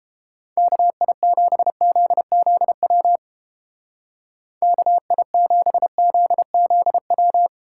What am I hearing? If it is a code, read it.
Morse code, heard as a keyed tone, about 31 wpm, 700 Hz. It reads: KI7ZZW KI7ZZW